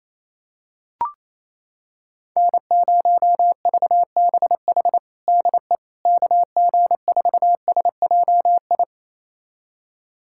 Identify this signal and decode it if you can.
Morse code, heard as a keyed tone, about 28 wpm, 700 Hz. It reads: N0VBH DE KG4SJI
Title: USA Callsign DE Callsign 28wpm